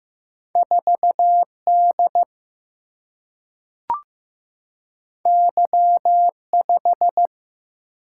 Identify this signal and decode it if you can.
Morse code, heard as a keyed tone, about 15 wpm, 700 Hz. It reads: 4D Y5
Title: Single-Letter Single-Number 15wpm